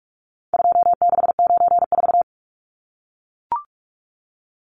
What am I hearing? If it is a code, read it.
Morse code, heard as a keyed tone, about 44 wpm, 700 Hz. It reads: 2694
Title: Numbers - 4-Digit 45wpm